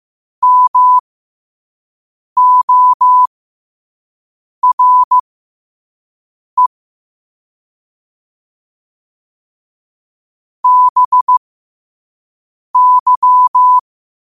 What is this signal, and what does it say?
Morse code, heard as a keyed tone, about 15 wpm, spaced as for 5 wpm, 1000 Hz. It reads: MORE BY
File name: qotdfiles / 5wpm